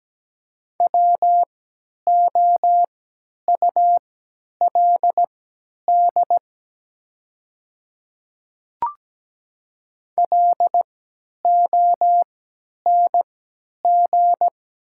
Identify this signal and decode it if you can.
Morse code, heard as a keyed tone, about 17 wpm, spaced as for 10 wpm, 700 Hz. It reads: WOULD LONG
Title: 3x Character Spacing - Top 200 Words - 17wpm